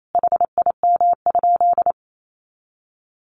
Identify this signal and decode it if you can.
Morse code, heard as a keyed tone, about 28 wpm, 700 Hz. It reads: HIM?
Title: Sentences from Top 100 Words 28wpm